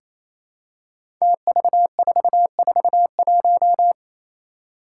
Morse code, heard as a keyed tone, about 28 wpm, 700 Hz. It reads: TV441
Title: Five Challenging Letter-Number Combinations 28wpm